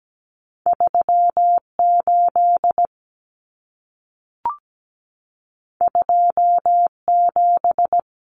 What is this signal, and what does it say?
Morse code, heard as a keyed tone, about 17 wpm, 700 Hz. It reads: 38 27
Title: Numbers - 2-Digit 17wpm